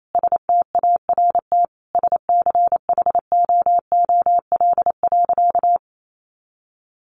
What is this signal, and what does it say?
Morse code, heard as a keyed tone, about 28 wpm, 700 Hz. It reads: START SCHOOL.
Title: Sentences from Top 1100 Words 28wpm